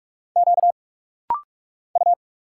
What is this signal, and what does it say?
Morse code, heard as a keyed tone, about 44 wpm, 690 Hz. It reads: Q U
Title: Single Letter-Number - Rapid-Fire 45wpm